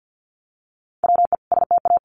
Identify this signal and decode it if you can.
Morse code, heard as a keyed tone, about 50 wpm, 700 Hz. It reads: WE STA
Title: Sentences from Top 1200 Words 50wpm